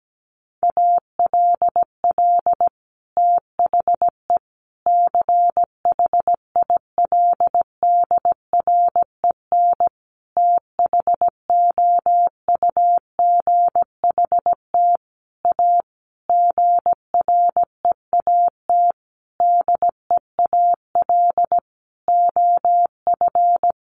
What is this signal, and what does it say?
Morse code, heard as a keyed tone, about 17 wpm, 700 Hz. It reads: ALL THE CHILDREN THOUGHT A GREAT DEAL OF
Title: Sentences from Top 800 Words 17wpm